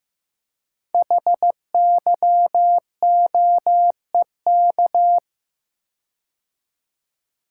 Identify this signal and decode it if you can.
Morse code, heard as a keyed tone, about 15 wpm, 700 Hz. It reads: HYOEK
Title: Code Groups - 5 Letters - 15wpm